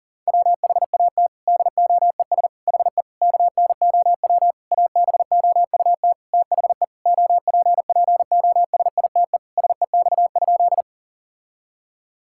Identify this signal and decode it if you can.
Morse code, heard as a keyed tone, about 40 wpm, 700 Hz. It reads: WHAT DOES HE KNOW ABOUT THE OPPOSITE SEX?
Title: Sentences from Top 1600 Words 40wpm